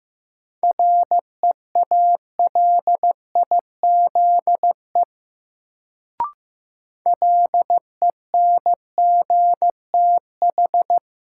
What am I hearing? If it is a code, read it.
Morse code, heard as a keyed tone, about 15 wpm, 700 Hz. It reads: REALIZE LENGTH
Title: Sentences from Top 1000 Words - New Word Drill - 15wpm